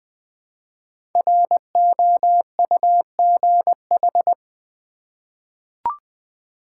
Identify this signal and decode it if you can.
Morse code, heard as a keyed tone, about 20 wpm, 700 Hz. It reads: ROUGH
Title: 5 Letter Words - First 3 Letters Given - 20wpm